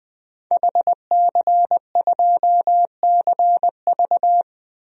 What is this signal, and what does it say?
Morse code, heard as a keyed tone, about 20 wpm, 700 Hz. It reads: HC2CV
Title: DXCC-100 Call Signs - 5 Characters - 20wpm